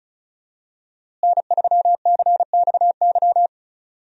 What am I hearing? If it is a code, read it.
Morse code, heard as a keyed tone, about 34 wpm, 700 Hz. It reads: N3CXY
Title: USA Call Signs - 5 Characters - Repeated 3x - 35wpm